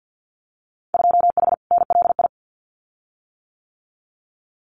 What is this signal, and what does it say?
Morse code, heard as a keyed tone, about 50 wpm, 700 Hz. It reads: 2H NLI
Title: Winter Field Day Exchanges Encore 50wpm